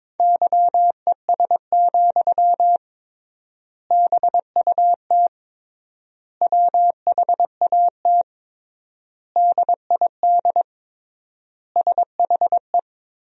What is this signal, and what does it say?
Morse code, heard as a keyed tone, about 22 wpm, 700 Hz. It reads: YES, BUT WHAT DID SHE
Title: 3x Word-Spacing - Sentences from Top 500 Words - 22wpm